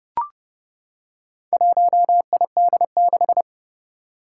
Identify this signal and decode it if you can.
Morse code, heard as a keyed tone, about 30 wpm, 690 Hz. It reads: 1ID6
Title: Four Challenging Letter-Number Combinations 30wpm